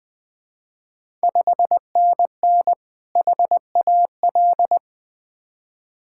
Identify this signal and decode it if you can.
Morse code, heard as a keyed tone, about 21 wpm, 700 Hz. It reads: 5NN HAL
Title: Ontario QSO Party Exchange 20wpm